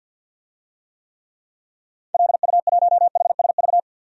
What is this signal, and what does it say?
Morse code, heard as a keyed tone, about 52 wpm, 700 Hz. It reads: LU1HSV
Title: DXCC-100 Call Signs - 6 Characters - 50wpm